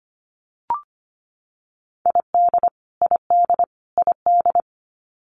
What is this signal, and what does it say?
Morse code, heard as a keyed tone, about 25 wpm, 700 Hz. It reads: ID ID ID
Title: US States Repeated 3x 25wpm